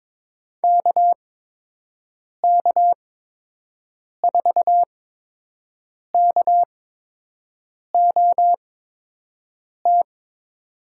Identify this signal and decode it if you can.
Morse code, heard as a keyed tone, about 21 wpm, spaced as for 6 wpm, 700 Hz. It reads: KK4KOT
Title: USA Call Signs - 6 Characters - 8x Character Spacing - 22wpm